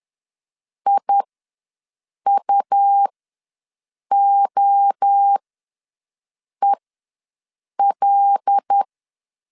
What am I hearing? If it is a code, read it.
Morse code, heard as a keyed tone, about 10 wpm, spaced as for 6 wpm, 800 Hz. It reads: IUOEL
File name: LEZIONI / 5 LBVC / 5 LBV ws 10wpm 3645